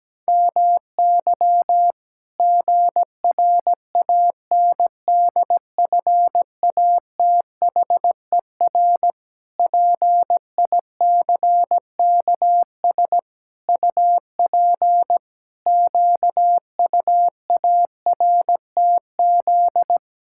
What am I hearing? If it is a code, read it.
Morse code, heard as a keyed tone, about 17 wpm, 700 Hz. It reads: MY GRANDFATHER PICKS UP QUARTZ
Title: Pangram Challenge v2 - 17wpm